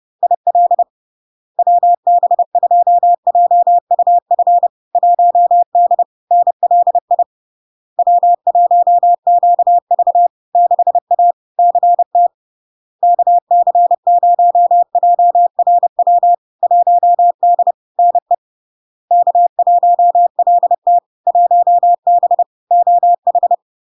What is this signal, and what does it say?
Morse code, heard as a keyed tone, about 30 wpm, 700 Hz. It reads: IL WB2JUF 1D NLI W1QV 6A CT KC0JRW 1D NE K1LT 1B OH
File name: LARK-FD-2022-Log-in-CW-30wpm-700-Hz